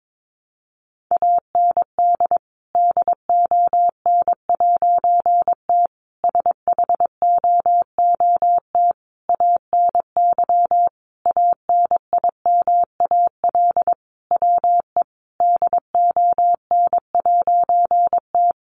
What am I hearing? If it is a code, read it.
Morse code, heard as a keyed tone, about 22 wpm, 700 Hz. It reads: AND DON'T SHOOT ANY ANIMAL WE DON'T
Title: A Sound Of Thunder 22wpm